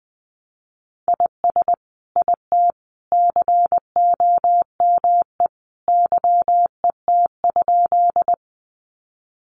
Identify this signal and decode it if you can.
Morse code, heard as a keyed tone, about 20 wpm, 700 Hz. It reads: IS IT COME YET?
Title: Sentences from Top 500 Words 20wpm